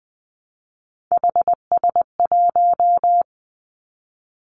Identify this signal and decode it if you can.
Morse code, heard as a keyed tone, about 19 wpm, 700 Hz. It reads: HS1
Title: Three Challenging Letter-Number Combinations 20wpm